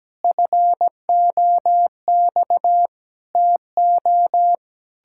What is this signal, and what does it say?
Morse code, heard as a keyed tone, about 17 wpm, 700 Hz. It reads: FOX TO
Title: Pangram Challenge v2 - 17wpm